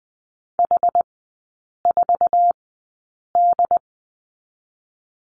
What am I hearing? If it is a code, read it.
Morse code, heard as a keyed tone, about 20 wpm, spaced as for 8 wpm, 690 Hz. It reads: H4D
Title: 093 Lesson 16e - Letter c - Three Characters 20wpm